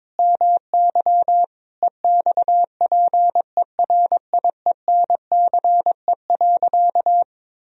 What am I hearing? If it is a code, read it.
Morse code, heard as a keyed tone, about 22 wpm, 700 Hz. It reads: MY EXPERIENCE.
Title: Sentences from Top 1000 Words - Review Part 3 - 22wpm